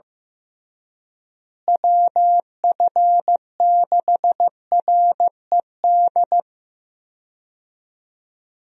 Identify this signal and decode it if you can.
Morse code, heard as a keyed tone, about 15 wpm, 700 Hz. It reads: WF6RED
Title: USA Call Signs - 6 Characters - Repeated 3x - 15wpm